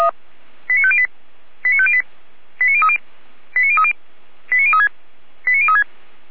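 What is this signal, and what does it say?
Five-tone selective call (ZVEI), heard as quick runs of five tones, tones of about 70 ms. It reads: 89508 89508 89030 89030 89036 89036
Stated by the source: ZVEI five-tone call 89050 received by SDR.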